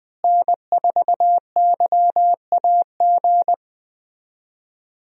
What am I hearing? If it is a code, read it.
Morse code, heard as a keyed tone, about 20 wpm, 700 Hz. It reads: N4YAG
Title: USA Call Signs - 5 Characters - Repeated 3x - 20wpm